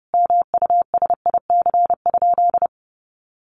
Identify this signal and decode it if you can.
Morse code, heard as a keyed tone, about 30 wpm, 700 Hz. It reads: MUSIC?
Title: Sentences from Top 1200 Words 30wpm